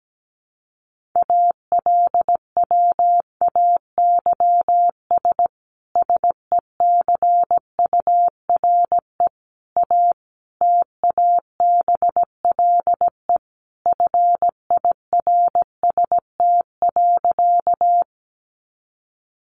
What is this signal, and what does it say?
Morse code, heard as a keyed tone, about 17 wpm, 700 Hz. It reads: ALWAYS SECURE A TABLE FIRST.
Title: Sentences from Top 1200 Words 17wpm